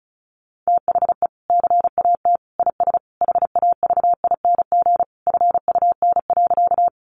Morse code, heard as a keyed tone, about 34 wpm, 700 Hz. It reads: THE CAT IS HAVING FUN.
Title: Easy Phrases 35wpm